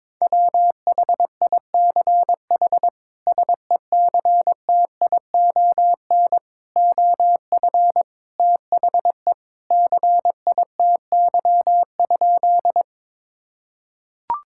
Morse code, heard as a keyed tone, about 22 wpm, 700 Hz. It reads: WHICH SECTION OF THE CITY?
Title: Sentences from Top 1000 Words - Review Part 5 - 22wpm